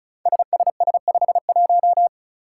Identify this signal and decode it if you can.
Morse code, heard as a keyed tone, about 36 wpm, 700 Hz. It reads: SSS51
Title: Five Challenging Letter-Number Combinations 35wpm